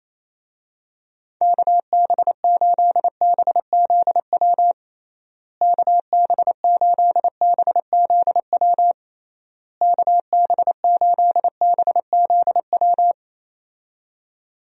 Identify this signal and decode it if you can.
Morse code, heard as a keyed tone, about 28 wpm, 700 Hz. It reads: KB8BZW KB8BZW KB8BZW
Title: USA Call Signs - 6 Characters - Repeated 3x - 28wpm